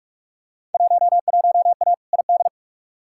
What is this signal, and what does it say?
Morse code, heard as a keyed tone, about 47 wpm, 700 Hz. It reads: 11A ID
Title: Field Day Exchanges Encore - 45wpm